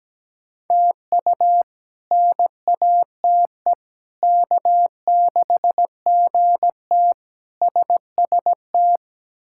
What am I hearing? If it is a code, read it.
Morse code, heard as a keyed tone, about 17 wpm, 700 Hz. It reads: TU NATE K6GT SST
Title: K1USN SST QSO Practice - 17wpm